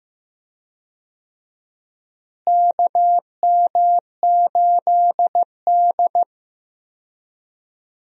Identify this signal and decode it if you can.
Morse code, heard as a keyed tone, about 15 wpm, 700 Hz. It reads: KM8D
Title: USA Call Signs - 4 Characters - Repeated 2x - 15wpm